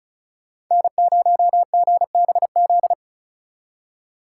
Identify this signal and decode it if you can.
Morse code, heard as a keyed tone, about 45 wpm, 700 Hz. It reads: N0GBZ